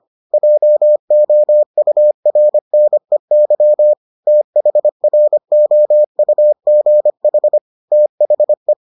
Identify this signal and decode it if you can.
Morse code, heard as a keyed tone, about 25 wpm, 600 Hz. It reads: JOURNEY THROUGH THE